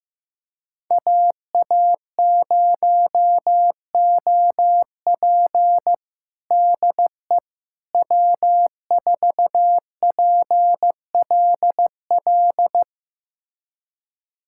Morse code, heard as a keyed tone, about 15 wpm, 700 Hz. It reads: AA0OP DE W4PLL
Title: USA Callsign DE Callsign - 5 Characters - 15wpm